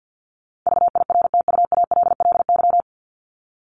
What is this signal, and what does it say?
Morse code, heard as a keyed tone, about 50 wpm, 700 Hz. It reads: VIRTUALLY
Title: Top 1501-1750 Words 50wpm